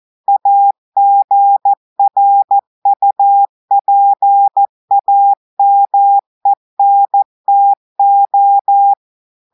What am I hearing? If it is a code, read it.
Morse code, heard as a keyed tone, about 14 wpm, 800 Hz. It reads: AGRUPAMENTO
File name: audios / portugais / portugais texte en clair 1 vitesse 14 mots minutes